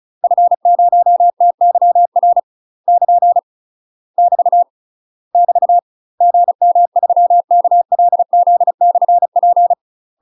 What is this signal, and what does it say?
Morse code, heard as a keyed tone, about 36 wpm, 700 Hz. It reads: F0TYR <KN> <BT> <BT> GM3KLZ/P
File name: UK MASTER QSOs TRAINING / 35WPM / UK MASTER QSOs 0041